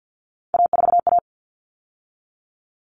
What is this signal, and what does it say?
Morse code, heard as a keyed tone, about 50 wpm, 700 Hz. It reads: A4A
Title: All Cut Numbers - 3-Digit 50wpm